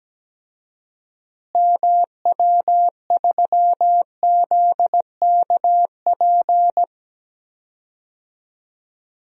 Morse code, heard as a keyed tone, about 17 wpm, 700 Hz. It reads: MW3ZKP
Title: DXCC-100 Call Signs - 6 Characters - Repeated 2x - 17wpm